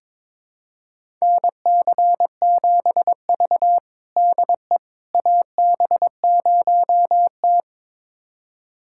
Morse code, heard as a keyed tone, about 22 wpm, 700 Hz. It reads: NC7V DE AB0T
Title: USA Callsign DE Callsign - 4 Characters - 22wpm